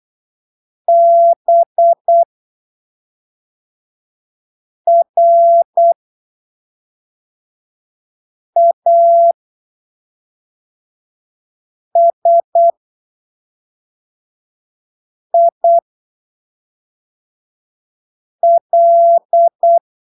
Morse code, heard as a keyed tone, about 8 wpm, spaced as for 3 wpm, 680 Hz. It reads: BRASIL
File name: arquivos / texto / texto 16 08ppm